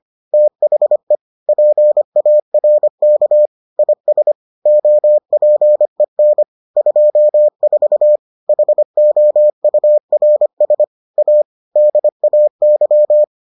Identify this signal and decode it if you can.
Morse code, heard as a keyed tone, about 25 wpm, 600 Hz. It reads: THE PARK IS OPEN 24 HOURS A DAY